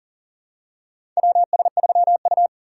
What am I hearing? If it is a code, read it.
Morse code, heard as a keyed tone, about 39 wpm, 700 Hz. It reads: WS3U
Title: USA Call Signs - 4 Characters - 2x Character Spacing - 40wpm